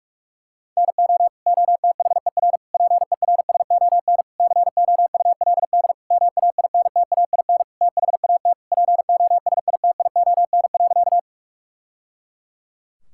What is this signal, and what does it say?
Morse code, heard as a keyed tone, about 42 wpm, 700 Hz. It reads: NO OTHER PERSON COULD MAINTAIN THAT POSITION.
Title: Sentences from Top 1000 Words - Review Part 5 - 45wpm